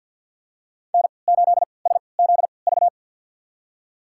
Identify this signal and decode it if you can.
Morse code, heard as a keyed tone, about 51 wpm, spaced as for 28 wpm, 700 Hz. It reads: N8SZV
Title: USA Call Signs - 5 Characters - 3x Character Spacing - 50wpm